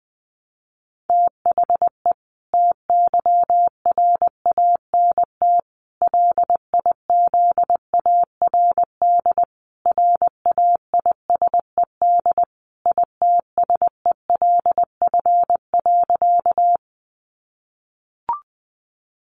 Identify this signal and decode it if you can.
Morse code, heard as a keyed tone, about 20 wpm, 700 Hz. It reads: THE TYRANT LIZARD RAISED ITSELF.
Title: A Sound Of Thunder 20wpm